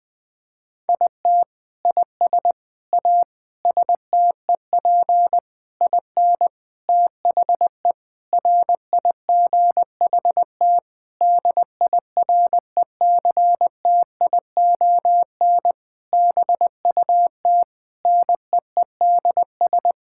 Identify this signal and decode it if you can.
Morse code, heard as a keyed tone, about 19 wpm, 700 Hz. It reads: IT IS A STEP IN THE RIGHT DIRECTION BUT NEEDS
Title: Sentences with 8-Word Phrases - Spoken Beginning - 20wpm